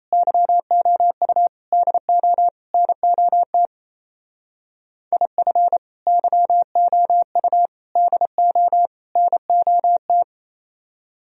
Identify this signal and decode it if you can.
Morse code, heard as a keyed tone, about 34 wpm, 700 Hz. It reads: YOU DO NOT IF YOU DO NOT
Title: Speed-Racing - Sets of 4 Words 28wpm